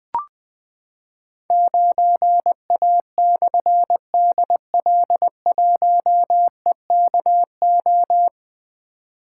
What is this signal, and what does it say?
Morse code, heard as a keyed tone, about 20 wpm, 700 Hz. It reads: 9A/DL1EKO
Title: Difficult Call Signs 20wpm